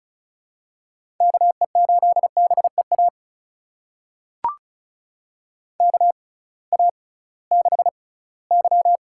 Morse code, heard as a keyed tone, about 34 wpm, spaced as for 20 wpm, 700 Hz. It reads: KE8BEA KA6Y
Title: USA Call Signs - 6 Characters - 6x Character Spacing - 35wpm